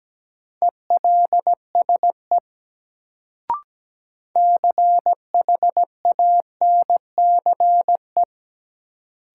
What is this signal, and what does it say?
Morse code, heard as a keyed tone, about 17 wpm, 700 Hz. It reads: ELSE CHANCE
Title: Sentences from Top 800 Words - New Word Drill - 17wpm